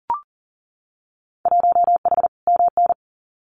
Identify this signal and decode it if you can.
Morse code, heard as a keyed tone, about 40 wpm, 700 Hz. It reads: 1H MN
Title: Winter Field Day Exchanges 40wpm